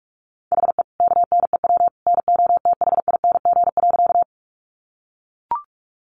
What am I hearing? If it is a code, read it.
Morse code, heard as a keyed tone, about 44 wpm, 700 Hz. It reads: HE KNEW NOTHING.
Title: Sentences from Top 500 Words 45wpm